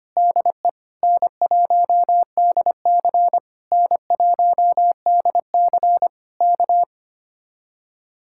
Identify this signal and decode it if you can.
Morse code, heard as a keyed tone, about 24 wpm, 700 Hz. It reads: DE N1DC N1DC K
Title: Speed-Racing Common QSO Phrases - 25wpm